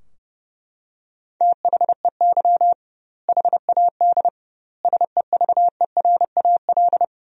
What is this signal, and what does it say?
Morse code, heard as a keyed tone, about 32 wpm, 700 Hz. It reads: THEY HAD SEVERAL
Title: 2x Word-Spacing - Sentences from Top 400 Words - 30wpm